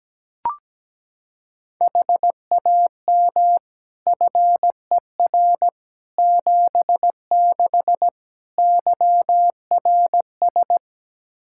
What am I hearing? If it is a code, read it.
Morse code, heard as a keyed tone, about 17 wpm, 700 Hz. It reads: HAM FER 76 YRS
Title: Practice QSO - Exchange 3 - More Info - 17wpm